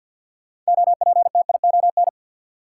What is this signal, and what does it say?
Morse code, heard as a keyed tone, about 53 wpm, 700 Hz. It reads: OPTION
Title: Sentences from Top 1300 Words - New Word Drill - 50wpm